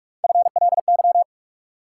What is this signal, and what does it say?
Morse code, heard as a keyed tone, about 45 wpm, 700 Hz. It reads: FLY
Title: Sentences from Top 500 Words - New Word Drill - 45wpm